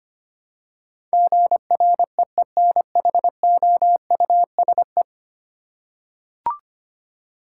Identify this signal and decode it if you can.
Morse code, heard as a keyed tone, about 25 wpm, 700 Hz. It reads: GREENHOUSE
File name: Ten Letter Words - 25wpm